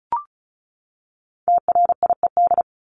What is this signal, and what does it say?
Morse code, heard as a keyed tone, about 34 wpm, 700 Hz. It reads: TRIED
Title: ED Suffix - Top 100 Words with Base Words 35wpm